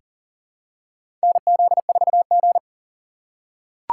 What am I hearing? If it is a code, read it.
Morse code, heard as a keyed tone, about 40 wpm, 700 Hz. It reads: NZ4G
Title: USA Call Signs - 4 Characters - 4x Character Spacing - 40wpm